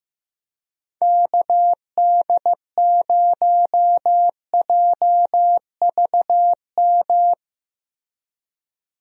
Morse code, heard as a keyed tone, about 15 wpm, 700 Hz. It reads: KD0JVM
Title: USA Call Signs - 6 Characters - Repeated 2x - 15wpm